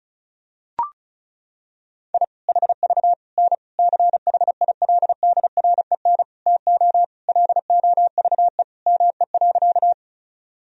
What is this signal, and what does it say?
Morse code, heard as a keyed tone, about 36 wpm, 700 Hz. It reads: I HV N CHILDREN TO LOVE ME.
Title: Sentences from Top 400 Words and Abbreviations 35wpm